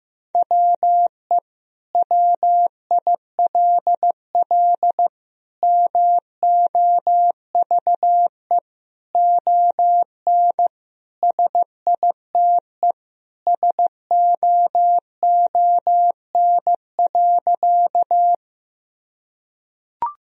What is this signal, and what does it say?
Morse code, heard as a keyed tone, about 15 wpm, 700 Hz. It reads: WE WILL MOVE ON SITE SOON.
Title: Sentences from Top 1000 Words - Review Part 4 - 15wpm